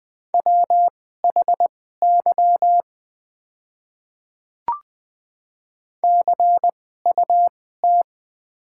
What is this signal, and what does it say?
Morse code, heard as a keyed tone, about 20 wpm, spaced as for 14 wpm, 700 Hz. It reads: WHY CUT
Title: 2x Character Spacing - Top 500 Words - 20wpm